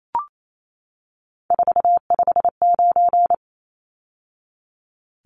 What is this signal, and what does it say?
Morse code, heard as a keyed tone, about 28 wpm, 700 Hz. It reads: 459
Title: Numbers - 3-Digit 28wpm